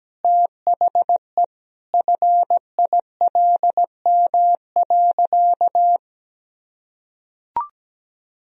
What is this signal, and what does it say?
Morse code, heard as a keyed tone, about 17 wpm, 700 Hz. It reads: THE FILM.
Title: Sentences from Top 1000 Words - Review Part 3 - 17wpm